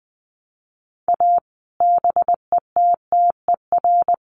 Morse code, heard as a keyed tone, about 20 wpm, 700 Hz. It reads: A BETTER
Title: Sets of 5 Words Encore2 20wpm